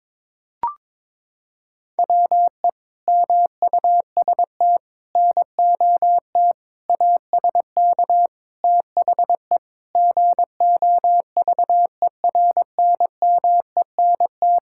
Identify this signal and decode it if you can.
Morse code, heard as a keyed tone, about 22 wpm, 700 Hz. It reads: WE MUST NOT ASK THE GOVERNMENT
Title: Sentences from Top 1600 Words 22wpm